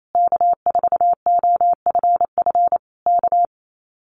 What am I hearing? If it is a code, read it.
Morse code, heard as a keyed tone, about 28 wpm, 700 Hz. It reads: K4OFF K
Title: Common QSO Phrases Encore 28wpm